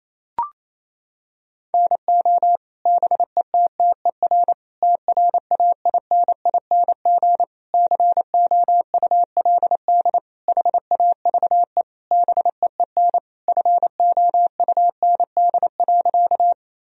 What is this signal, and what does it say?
Morse code, heard as a keyed tone, about 28 wpm, 700 Hz. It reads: NO BETTER TRAINING COULD HAVE BEEN FOUND.
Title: Sentences from Top 1000 Words - Review Part 4 - 28wpm